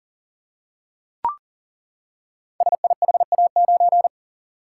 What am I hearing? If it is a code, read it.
Morse code, heard as a keyed tone, about 39 wpm, 690 Hz. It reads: SIHA9
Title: Code Groups - 5 Letter:Numbers - 40wpm